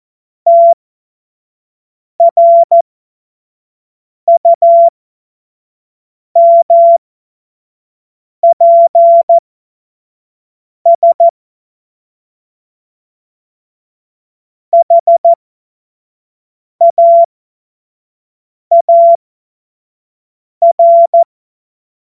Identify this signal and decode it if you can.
Morse code, heard as a keyed tone, about 14 wpm, spaced as for 5 wpm, 680 Hz. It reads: TRUMPS HAAR
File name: Morsekurs / Witze / Witz 09. November 2021, 25 F70